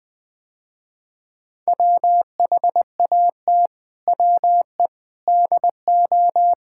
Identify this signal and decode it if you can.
Morse code, heard as a keyed tone, about 20 wpm, 700 Hz. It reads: WHAT WE DO
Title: Sentences with 7-Word Phrases - Spoken Beginning - 20wpm